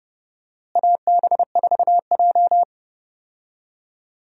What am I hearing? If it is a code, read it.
Morse code, heard as a keyed tone, about 30 wpm, 700 Hz. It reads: AB4J
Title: USA Call Signs - 4 Characters - Repeated 2x - 30wpm